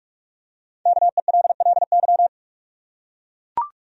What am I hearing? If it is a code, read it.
Morse code, heard as a keyed tone, about 42 wpm, 700 Hz. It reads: KELLY
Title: USA Common Names - Max 5 Letters - 45wpm